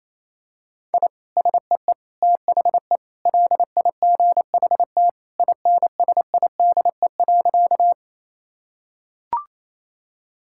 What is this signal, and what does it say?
Morse code, heard as a keyed tone, about 27 wpm, 700 Hz. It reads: I SEE THE LIGHT INSIDE.
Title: Sentences from Top 1000 Words - Review Part 6 - 28wpm